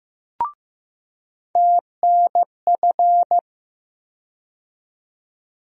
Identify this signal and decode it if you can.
Morse code, heard as a keyed tone, about 15 wpm, 700 Hz. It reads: TNF
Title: USA Call Sign Postfixes - 3 Letters 15wpm